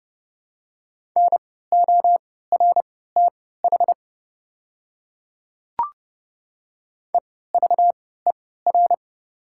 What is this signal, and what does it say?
Morse code, heard as a keyed tone, about 30 wpm, spaced as for 17 wpm, 690 Hz. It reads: NORTH EVER
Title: 3x Character Spacing - Top 500 Words - 30wpm